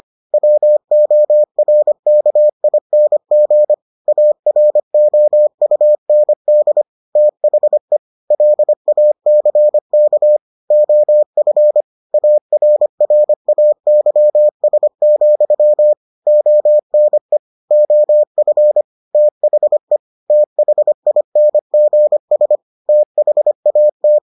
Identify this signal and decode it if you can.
Morse code, heard as a keyed tone, about 25 wpm, 600 Hz. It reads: WORKING AROUND THE LACK OF ARRAYS, ONE OF THE THINGS THAT